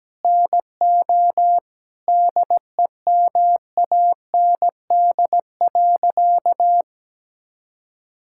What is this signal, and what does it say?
Morse code, heard as a keyed tone, about 17 wpm, 700 Hz. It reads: NO DEMAND.
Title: Sentences from Top 1000 Words - Review Part 5 - 17wpm